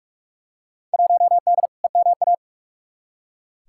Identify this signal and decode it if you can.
Morse code, heard as a keyed tone, about 43 wpm, 700 Hz. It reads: 1D EMA
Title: Field Day Exchanges - 45wpm